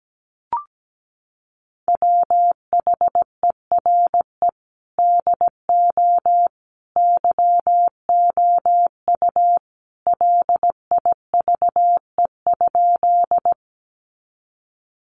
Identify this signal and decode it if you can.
Morse code, heard as a keyed tone, about 17 wpm, 700 Hz. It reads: WHERE DO YOU LIVE?
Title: Sentences from Top 200 Words 17wpm